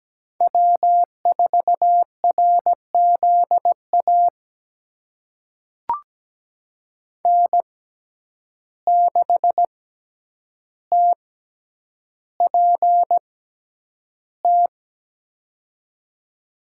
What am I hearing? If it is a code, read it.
Morse code, heard as a keyed tone, about 17 wpm, spaced as for 9 wpm, 700 Hz. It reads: W4RZA N6TPT
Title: USA Call Signs - 5 Characters - 6x Character Spacing - 17wpm